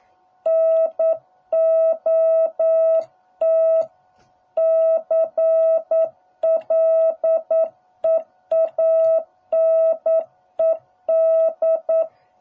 Morse code, heard as a keyed tone, about 9 wpm, 650 Hz. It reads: NOT CLEANED